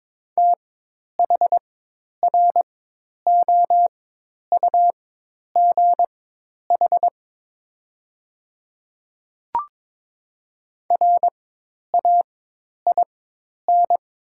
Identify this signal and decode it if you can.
Morse code, heard as a keyed tone, about 22 wpm, spaced as for 10 wpm, 700 Hz. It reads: THROUGH RAIN
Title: 4x Character Spacing - Top 300 Words - 22wpm